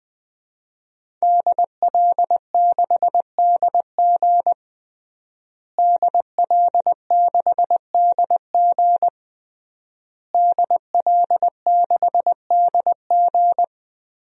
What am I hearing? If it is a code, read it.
Morse code, heard as a keyed tone, about 20 wpm, 700 Hz. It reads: DL6DG DL6DG DL6DG
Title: DXCC-100 Call Signs - 5 Characters - Repeated 3x - 20wpm